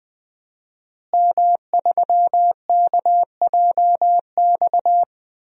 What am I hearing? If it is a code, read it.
Morse code, heard as a keyed tone, about 20 wpm, 700 Hz. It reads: M3KJX
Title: DXCC-100 Call Signs - 5 Characters - Repeated 3x - 20wpm